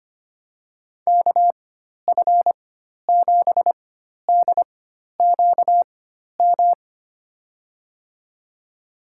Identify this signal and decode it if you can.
Morse code, heard as a keyed tone, about 25 wpm, spaced as for 12 wpm, 700 Hz. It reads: KF7DQM
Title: USA Call Signs - 6 Characters - 4x Character Spacing - 25wpm